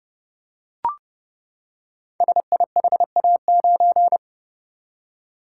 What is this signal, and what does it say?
Morse code, heard as a keyed tone, about 30 wpm, 690 Hz. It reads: SIHA9
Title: Code Groups - 5 Letter:Numbers - 30wpm